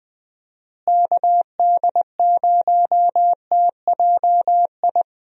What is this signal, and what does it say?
Morse code, heard as a keyed tone, about 19 wpm, 700 Hz. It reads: KD0TJI
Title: USA Call Signs - 6 Characters - Repeated 3x - 20wpm